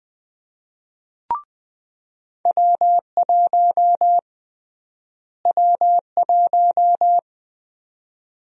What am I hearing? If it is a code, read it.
Morse code, heard as a keyed tone, about 21 wpm, 700 Hz. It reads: W1 W1
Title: USA Call Sign Prefixes - Repeated 2x - 20wpm